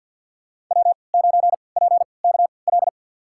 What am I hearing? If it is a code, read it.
Morse code, heard as a keyed tone, about 47 wpm, spaced as for 28 wpm, 700 Hz. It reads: W9PKL
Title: USA Call Signs - 5 Characters - 3x Character Spacing - 50wpm